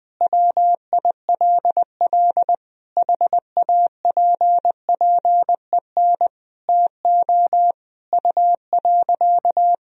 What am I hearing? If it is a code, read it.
Morse code, heard as a keyed tone, about 20 wpm, 700 Hz. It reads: WILL HAPPEN TO U.
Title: Sentences from Top 500 Words and Abbreviations 20wpm